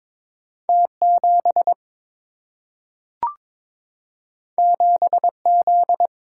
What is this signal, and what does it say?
Morse code, heard as a keyed tone, about 22 wpm, 700 Hz. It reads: T 7 7 Z
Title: Single-Letter Single-Number 22wpm